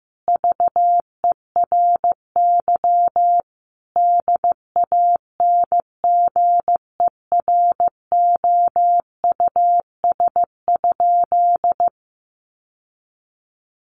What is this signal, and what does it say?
Morse code, heard as a keyed tone, about 15 wpm, 700 Hz. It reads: VERY DANGEROUS?
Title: Sentences from Top 1200 Words 15wpm